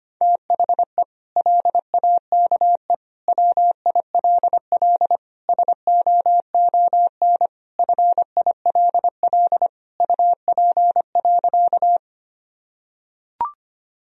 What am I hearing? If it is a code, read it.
Morse code, heard as a keyed tone, about 25 wpm, 700 Hz. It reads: THE LAKE WILL SOON FILL UP.
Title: Sentences from Top 1000 Words - Review Part 3 - 25wpm